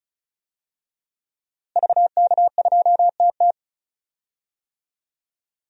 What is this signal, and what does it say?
Morse code, heard as a keyed tone, about 34 wpm, 700 Hz. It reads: VK2TT
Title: DXCC-100 Call Signs - 5 Characters - Repeated 3x - 35wpm